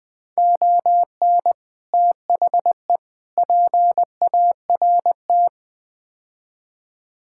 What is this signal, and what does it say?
Morse code, heard as a keyed tone, about 20 wpm, 700 Hz. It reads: ON THE PART
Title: Build A Phrase - 5-Word Phrases - 20wpm